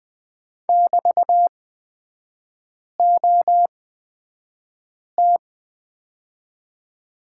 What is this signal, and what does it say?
Morse code, heard as a keyed tone, about 20 wpm, spaced as for 5 wpm, 690 Hz. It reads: <BT>OT
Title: 229 Lesson 37e - Letter x - Three Characters 20wpm